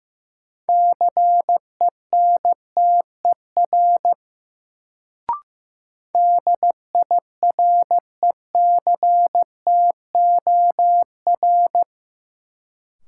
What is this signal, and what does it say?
Morse code, heard as a keyed tone, about 15 wpm, 700 Hz. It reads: CENTER DIRECTOR
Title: Sets of 3 Words - New Word Drill - 15wpm